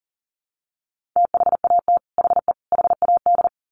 Morse code, heard as a keyed tone, about 40 wpm, 700 Hz. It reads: THAT HE HAD
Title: Sets of 3 Words 40wpm